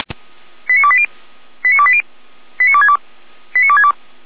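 Five-tone selective call (ZVEI), heard as quick runs of five tones, tones of about 70 ms. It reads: 89280 89280 89262 89262
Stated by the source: ZVEI five-tone selective call received on an SDR, from a fire brigade (Feuerwehr) in Germany.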